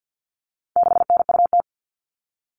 Morse code, heard as a keyed tone, about 50 wpm, 700 Hz. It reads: 6NUT
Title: All Cut Numbers - 4-Digit 50wpm